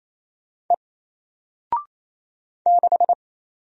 Morse code, heard as a keyed tone, about 28 wpm, 700 Hz. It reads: E 6
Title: Single Character (Letter:Number:Punctuation:Prosign) 28wpm